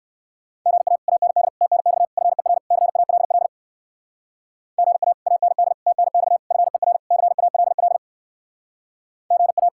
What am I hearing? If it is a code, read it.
Morse code, heard as a keyed tone, about 58 wpm, 700 Hz. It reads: GA AND TNX FER CALL GA AND TNX FER CALL GA